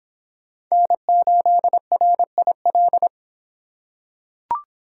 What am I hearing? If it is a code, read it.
Morse code, heard as a keyed tone, about 23 wpm, 700 Hz. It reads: N8RIL